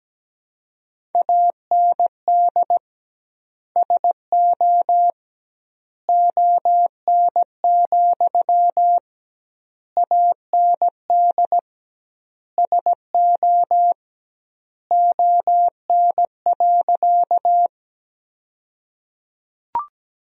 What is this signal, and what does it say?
Morse code, heard as a keyed tone, about 17 wpm, 700 Hz. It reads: AND SO ON, AND SO ON.
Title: 2x Word-Spacing - Sentences from Top 100 Words - 17wpm